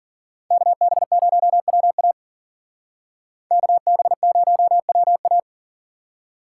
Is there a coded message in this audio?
Yes: Morse code from 0.5 to 5.4 s, heard as a keyed tone, about 52 wpm, 700 Hz. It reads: KB0WA KB0WA